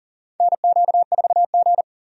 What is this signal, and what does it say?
Morse code, heard as a keyed tone, about 40 wpm, 700 Hz. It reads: NQ4G
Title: USA Call Signs - 4 Characters - 40wpm